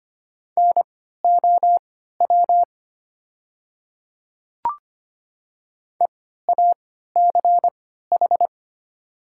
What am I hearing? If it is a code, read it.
Morse code, heard as a keyed tone, about 25 wpm, spaced as for 14 wpm, 700 Hz. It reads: NOW EACH
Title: 3x Character Spacing - Top 100 Words - 25wpm